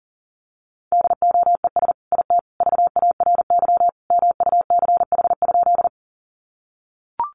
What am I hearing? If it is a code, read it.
Morse code, heard as a keyed tone, about 40 wpm, 700 Hz. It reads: DOES IT VARY MUCH?
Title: Sentences from Top 1200 Words 40wpm